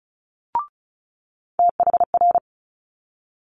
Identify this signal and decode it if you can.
Morse code, heard as a keyed tone, about 36 wpm, 700 Hz. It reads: THR
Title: Multi-letter Phonemes Quick Fire 35wpm